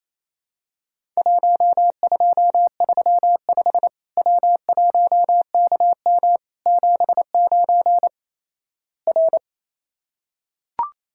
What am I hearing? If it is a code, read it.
Morse code, heard as a keyed tone, about 28 wpm, 700 Hz. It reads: 1235 W1KM 79 R
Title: Worked All Europe DX Contest - QTC Exchange - 28wpm